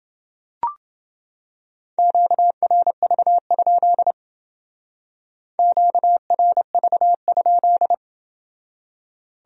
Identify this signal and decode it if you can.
Morse code, heard as a keyed tone, about 28 wpm, 700 Hz. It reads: QRV? QRV?